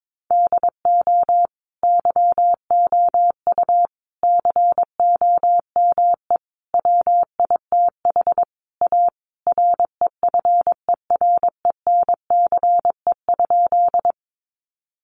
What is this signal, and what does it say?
Morse code, heard as a keyed tone, about 22 wpm, 700 Hz. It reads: DO YOU COME WITH A REFERENCE?
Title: Sentences from Top 1200 Words 22wpm